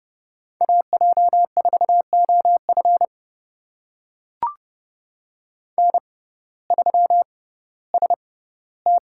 Morse code, heard as a keyed tone, about 30 wpm, spaced as for 17 wpm, 700 Hz. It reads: AJ4OF N3ST
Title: USA Call Signs - 5 Characters - 6x Character Spacing - 30wpm